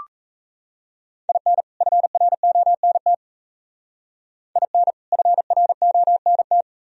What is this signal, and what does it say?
Morse code, heard as a keyed tone, about 36 wpm, 700 Hz. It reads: IN FRONT IN FRONT